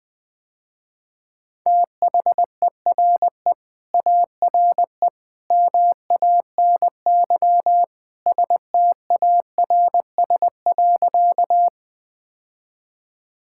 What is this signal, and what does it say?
Morse code, heard as a keyed tone, about 21 wpm, 700 Hz. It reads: THERE ARE MANY STARS.
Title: Sentences with 2-Word Phrases - Spoken Beginning - 20wpm